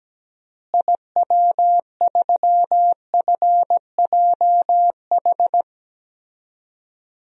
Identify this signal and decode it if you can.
Morse code, heard as a keyed tone, about 17 wpm, 700 Hz. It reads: IW3FJH
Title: DXCC-100 Call Signs - 6 Characters - Repeated 3x - 17wpm